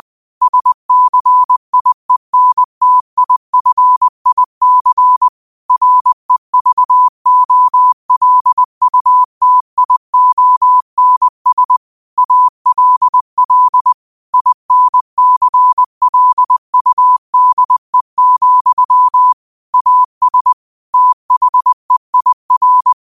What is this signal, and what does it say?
Morse code, heard as a keyed tone, about 20 wpm, 1000 Hz. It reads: SCIENTIFIC REVOLUTIONS ALL INCLUDE, AS THEIR